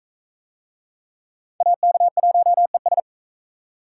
Morse code, heard as a keyed tone, about 38 wpm, 700 Hz. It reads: AK1ES